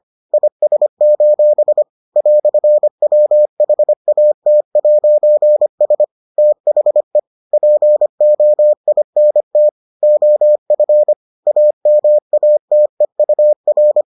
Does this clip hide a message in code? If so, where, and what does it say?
Yes: Morse code from 0.3 to 14.0 s, heard as a keyed tone, about 25 wpm, 600 Hz. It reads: IS: "WHAT'S THE POINT OF AMATEUR